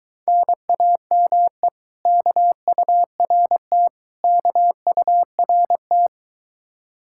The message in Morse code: NAME KURT KURT

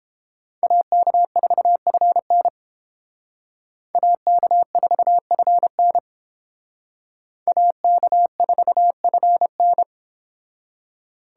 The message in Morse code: AK4FN AK4FN AK4FN

